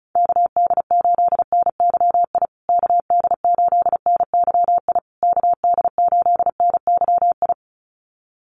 Morse code, heard as a keyed tone, about 34 wpm, 700 Hz. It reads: KD8NYI KD8NYI KD8NYI
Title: USA Call Signs Repeated 3x 35wpm